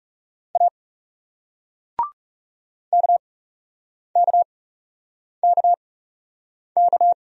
Morse code, heard as a keyed tone, about 40 wpm, spaced as for 8 wpm, 700 Hz. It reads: A KKKK